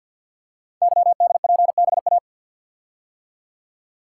Morse code, heard as a keyed tone, about 49 wpm, 700 Hz. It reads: YDPBA
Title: Code Groups - 5 Letters - 50wpm